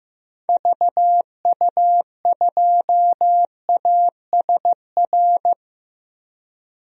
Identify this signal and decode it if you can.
Morse code, heard as a keyed tone, about 15 wpm, 700 Hz. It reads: VU2ASR
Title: DXCC-100 Call Signs - 6 Characters - Repeated 2x - 15wpm